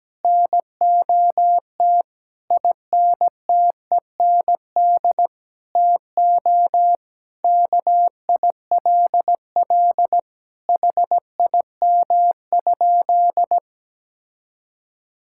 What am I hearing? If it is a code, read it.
Morse code, heard as a keyed tone, about 17 wpm, 700 Hz. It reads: NOT INTEND TO KILL HIM?
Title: Sentences from Top 1600 Words 17wpm